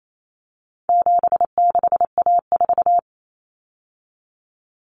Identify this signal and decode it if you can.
Morse code, heard as a keyed tone, about 28 wpm, 700 Hz. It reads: 76A4
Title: Common Cut Numbers - 4-Digit 28wpm